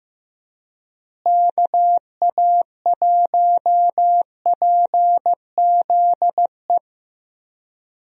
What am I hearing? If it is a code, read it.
Morse code, heard as a keyed tone, about 15 wpm, 700 Hz. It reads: KA1PZE
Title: USA Call Signs - 6 Characters - Repeated 3x - 15wpm